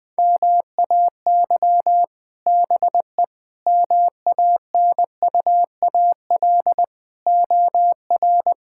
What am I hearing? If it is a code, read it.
Morse code, heard as a keyed tone, about 20 wpm, 700 Hz. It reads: MAY BE MANUAL OR
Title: Difficult Words - Speed-Racing with Hints - 20wpm